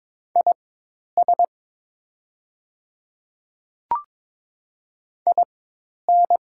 Morse code, heard as a keyed tone, about 22 wpm, spaced as for 10 wpm, 700 Hz. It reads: IS IN
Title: 4x Character Spacing - Top 100 Words - 22wpm